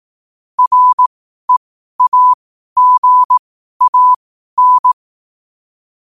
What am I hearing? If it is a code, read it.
Morse code, heard as a keyed tone, about 18 wpm, spaced as for 12 wpm, 1000 Hz. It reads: REAGAN
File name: qotdfiles / 12wpm